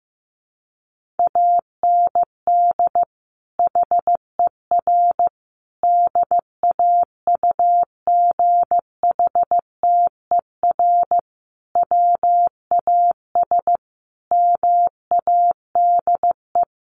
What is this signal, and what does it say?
Morse code, heard as a keyed tone, about 15 wpm, 700 Hz. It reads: AND HER DAUGHTER WAS MADE
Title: Sentences from Top 1500 Words 15wpm